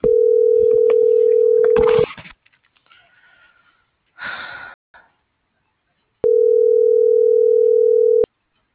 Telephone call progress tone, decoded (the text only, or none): ringback tone